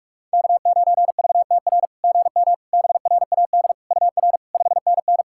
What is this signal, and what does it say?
Morse code, heard as a keyed tone, about 43 wpm, 690 Hz. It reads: K9VTR GM BRAD UR 5NN
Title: SOTA QSO Practice 45wpm